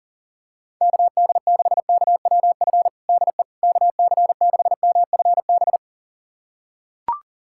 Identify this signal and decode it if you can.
Morse code, heard as a keyed tone, about 37 wpm, 690 Hz. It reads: KD6KWF DE KC6MFB
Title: USA Callsign DE Callsign - 6 Characters - 40wpm